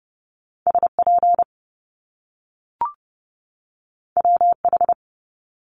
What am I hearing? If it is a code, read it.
Morse code, heard as a keyed tone, about 30 wpm, 700 Hz. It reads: SP WH
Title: Top Two Letter Combinations in Words 30wpm